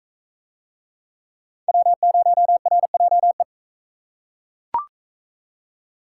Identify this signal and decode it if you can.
Morse code, heard as a keyed tone, about 38 wpm, 700 Hz. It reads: W0RJE